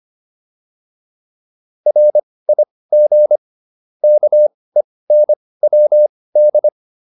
Morse code, heard as a keyed tone, about 25 wpm, spaced as for 18 wpm, 600 Hz. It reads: RIG KENWD
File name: QSO308 18